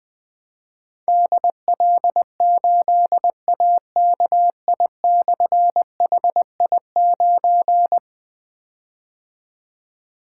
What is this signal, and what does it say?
Morse code, heard as a keyed tone, about 20 wpm, 700 Hz. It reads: DL8AKI/HI9
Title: Difficult Call Signs 20wpm